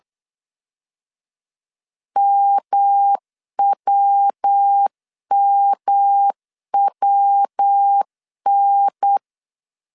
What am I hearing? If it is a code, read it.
Morse code, heard as a keyed tone, about 8 wpm, 800 Hz. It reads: MWMWN
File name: CW / DET / 1 3 1500C S A 8wpm 4229